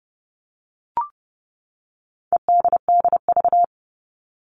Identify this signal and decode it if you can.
Morse code, heard as a keyed tone, about 30 wpm, 700 Hz. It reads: EDDV
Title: All Cut Numbers - 4-Digit 30wpm